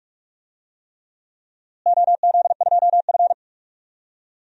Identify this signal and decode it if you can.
Morse code, heard as a keyed tone, about 46 wpm, 700 Hz. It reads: OZ2F
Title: DXCC-100 Call Signs - 4 Characters - Repeated 3x - 45wpm